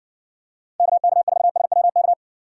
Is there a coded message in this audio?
Yes: Morse code from 0.8 to 2.1 s, heard as a keyed tone, about 54 wpm, 700 Hz. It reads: KK4SWK